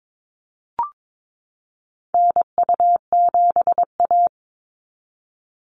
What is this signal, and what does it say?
Morse code, heard as a keyed tone, about 22 wpm, 700 Hz. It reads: NU7A
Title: NA SOTA Activator Callsigns 22wpm